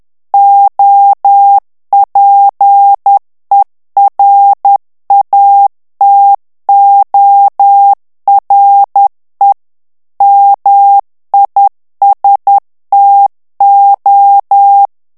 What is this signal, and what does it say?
Morse code, heard as a keyed tone, about 10 wpm, 800 Hz. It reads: OPERATORE MISTO
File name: ITALIEN texte clair 2 vitesse 10 mots minute